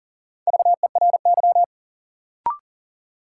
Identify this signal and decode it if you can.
Morse code, heard as a keyed tone, about 40 wpm, 700 Hz. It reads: VERY
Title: Sets of 2 Words - New Word Drill - 40wpm